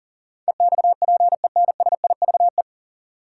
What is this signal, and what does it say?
Morse code, heard as a keyed tone, about 40 wpm, 700 Hz. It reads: EXPENSIVE
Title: Sentences from Top 1500 Words - New Word Drill Alt - 40wpm